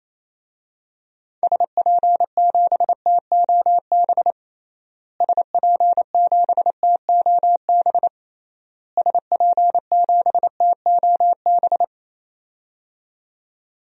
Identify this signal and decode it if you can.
Morse code, heard as a keyed tone, about 28 wpm, 700 Hz. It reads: SP7TOB SP7TOB SP7TOB
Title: DXCC-100 Call Signs - 6 Characters - Repeated 3x - 28wpm